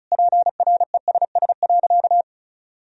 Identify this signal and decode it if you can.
Morse code, heard as a keyed tone, about 34 wpm, 700 Hz. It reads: PRESS.
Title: Sentences from Top 1000 Words - Review Part 1 - 35wpm